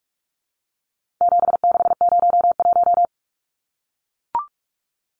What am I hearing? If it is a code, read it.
Morse code, heard as a keyed tone, about 44 wpm, 700 Hz. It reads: 7601
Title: Numbers - 4-Digit 45wpm